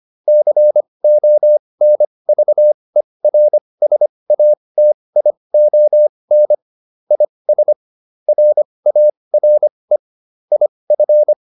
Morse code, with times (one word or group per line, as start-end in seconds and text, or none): 0.3-6.5 CONVERSATION
7.1-7.7 IS
8.3-10.0 RARE
10.5-11.3 IF